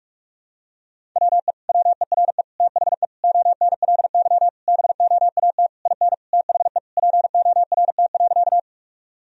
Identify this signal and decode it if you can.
Morse code, heard as a keyed tone, about 46 wpm, 700 Hz. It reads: WE WERE THE ONLY BOAT IN THE PORT.
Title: Sentences from Top 1000 Words - Review Part 1 - 45wpm